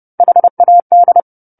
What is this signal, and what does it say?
Morse code, heard as a keyed tone, about 30 wpm, 700 Hz. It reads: HAD
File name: 10-Chapter-10 30 WPM